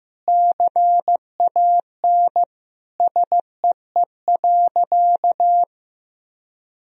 Morse code, heard as a keyed tone, about 15 wpm, 700 Hz. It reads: CAN SEE.
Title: Sentences from Top 1000 Words - Review Part 1 - 15wpm